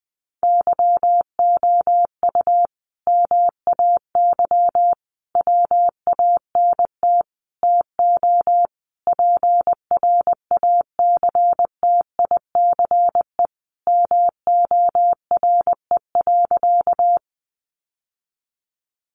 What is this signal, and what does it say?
Morse code, heard as a keyed tone, about 20 wpm, 700 Hz. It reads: YOU MAY WANT TO PRACTICE MORE.
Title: Sentences from Top 700 Words 20wpm